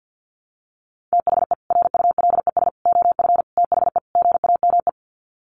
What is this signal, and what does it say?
Morse code, heard as a keyed tone, about 50 wpm, 700 Hz. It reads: THE RULES OF THE GAME
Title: Sets of 5 Words Encore2 50wpm